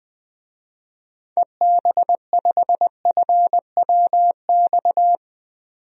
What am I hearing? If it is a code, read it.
Morse code, heard as a keyed tone, about 20 wpm, 700 Hz. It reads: EB5FWX
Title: DXCC-100 Call Signs - 6 Characters - 20wpm